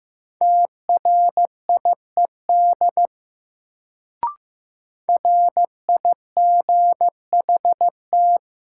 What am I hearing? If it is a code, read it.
Morse code, heard as a keyed tone, about 15 wpm, 700 Hz. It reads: TRIED RIGHT
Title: Sets of 2 Words - New Word Drill - 15wpm